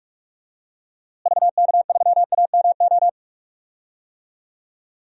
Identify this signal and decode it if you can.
Morse code, heard as a keyed tone, about 46 wpm, 700 Hz. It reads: VK3AMO
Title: DXCC-100 Call Signs - 6 Characters - Repeated 3x - 45wpm